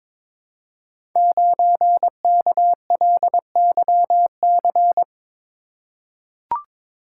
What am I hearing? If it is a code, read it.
Morse code, heard as a keyed tone, about 22 wpm, 700 Hz. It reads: 9KLYC
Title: Code Groups - 5 Letter:Numbers - 22wpm